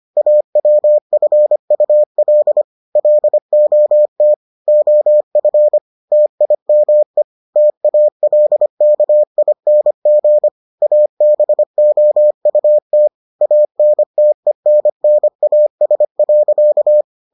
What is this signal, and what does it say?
Morse code, heard as a keyed tone, about 25 wpm, 600 Hz. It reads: AWFUL LOT OF TIME TALKING ABOUT ANTENNAS.